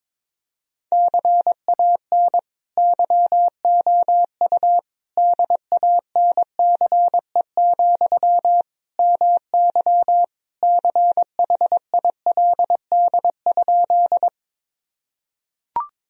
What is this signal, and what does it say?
Morse code, heard as a keyed tone, about 22 wpm, 700 Hz. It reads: CAN YOU DANCE, MY CHILD?
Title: Sentences from Top 1000 Words - Review Part 4 - 22wpm